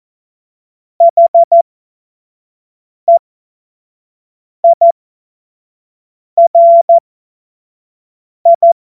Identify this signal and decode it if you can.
Morse code, heard as a keyed tone, about 14 wpm, spaced as for 5 wpm, 680 Hz. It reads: HEIRI
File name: Witz-26-August-2025-25-F70